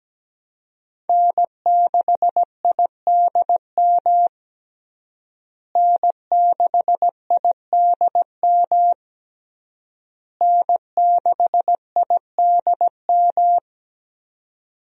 Morse code, heard as a keyed tone, about 17 wpm, 700 Hz. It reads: N6IDM N6IDM N6IDM
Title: USA Call Signs - 5 Characters - Repeated 3x - 17wpm